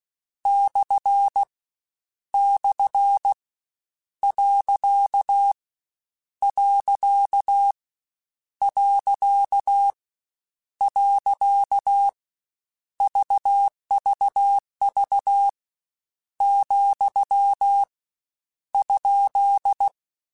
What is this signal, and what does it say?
Morse code, heard as a keyed tone, about 16 wpm, 800 Hz. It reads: / / . . VVV , ?
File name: Misc / Misc-mp3s / Punct @15